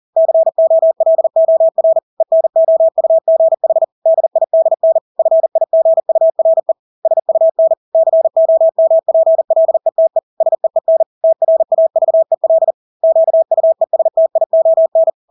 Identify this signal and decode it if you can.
Morse code, heard as a keyed tone, about 40 wpm, 650 Hz. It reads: COLOR ENOUGH DIDN FIGURE SUN COMPLETE SEEN TRAVEL QUESTION
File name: Engelske ord 40-40 wpm